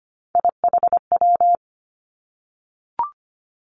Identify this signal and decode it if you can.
Morse code, heard as a keyed tone, about 25 wpm, 700 Hz. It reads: IHW
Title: Three Challenging Letter-Number Combinations 25wpm